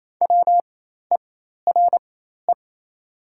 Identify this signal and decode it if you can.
Morse code, heard as a keyed tone, about 28 wpm, spaced as for 13 wpm, 690 Hz. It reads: WERE
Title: 4x Character Spacing - Top 300 Words - 28wpm